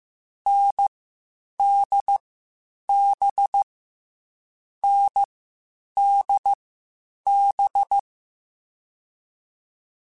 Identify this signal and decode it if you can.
Morse code, heard as a keyed tone, about 15 wpm, spaced as for 8 wpm, 800 Hz. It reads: NDB NDB
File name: Difficult Combos / Difficult Combos-mp3s / NDB NDB6 @13